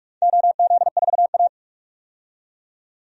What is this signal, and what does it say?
Morse code, heard as a keyed tone, about 41 wpm, 700 Hz. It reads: OZ4A
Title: DXCC-100 Call Signs - 4 Characters - 45wpm